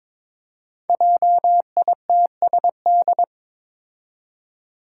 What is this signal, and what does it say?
Morse code, heard as a keyed tone, about 22 wpm, 700 Hz. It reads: JITSD
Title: Five Challenging Letter-Number Combinations 22wpm